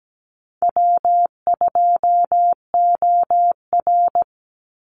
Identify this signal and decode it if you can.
Morse code, heard as a keyed tone, about 17 wpm, 700 Hz. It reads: W2OR
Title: Easy Call Signs - 17wpm